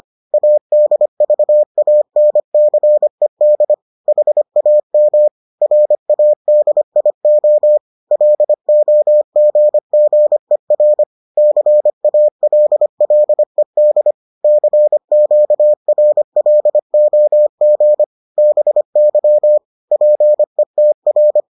Morse code, heard as a keyed tone, about 25 wpm, 600 Hz. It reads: ADVANCED HAM RADIO LOGGER CALLED CQRLOG BY PETR